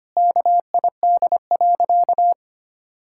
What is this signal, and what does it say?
Morse code, heard as a keyed tone, about 25 wpm, 700 Hz. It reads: KID.
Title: Sentences from Top 1600 Words 25wpm